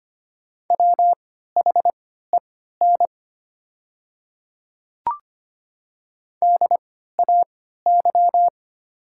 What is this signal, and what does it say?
Morse code, heard as a keyed tone, about 25 wpm, spaced as for 14 wpm, 700 Hz. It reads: WHEN DAY
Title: 3x Character Spacing - Top 100 Words - 25wpm